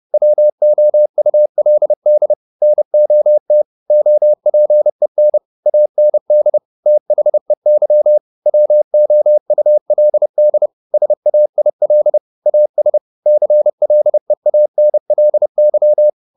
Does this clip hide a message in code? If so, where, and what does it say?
Yes: Morse code from 0.1 to 16.1 s, heard as a keyed tone, about 30 wpm, 600 Hz. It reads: WOULD NOT OPEN AND THEY WOULD SAIL AS CLEANLY